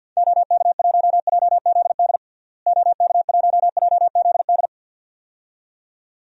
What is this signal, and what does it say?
Morse code, heard as a keyed tone, about 52 wpm, 700 Hz. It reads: OK1JZD OK1JZD
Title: DXCC-100 Call Signs - 6 Characters - Repeated 2x - 50wpm